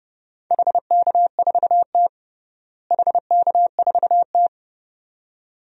Morse code, heard as a keyed tone, about 30 wpm, 700 Hz. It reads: HK4T HK4T
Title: DXCC-100 Call Signs - 4 Characters - Repeated 2x - 30wpm